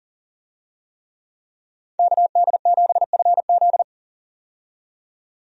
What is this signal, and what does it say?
Morse code, heard as a keyed tone, about 37 wpm, 700 Hz. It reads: KD7FZ
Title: USA Call Signs - 5 Characters - 40wpm